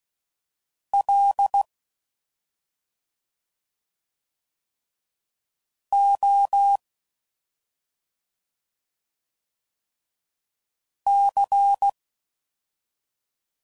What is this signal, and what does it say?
Morse code, heard as a keyed tone, about 16 wpm, 800 Hz. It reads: L O C